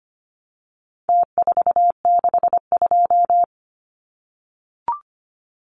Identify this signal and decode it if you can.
Morse code, heard as a keyed tone, about 25 wpm, 700 Hz. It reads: T462
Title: Numbers - Contest Style - 4-Digit 25wpm